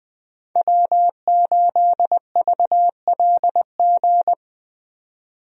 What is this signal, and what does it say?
Morse code, heard as a keyed tone, about 20 wpm, 700 Hz. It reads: W8VLG
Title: USA Call Signs - 5 Characters - 20wpm